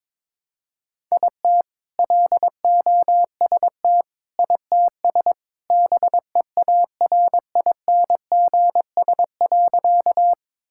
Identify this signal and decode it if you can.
Morse code, heard as a keyed tone, about 22 wpm, 700 Hz. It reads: IT LOST ITS BEARINGS.
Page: morse-code-ninja-practice-22wpm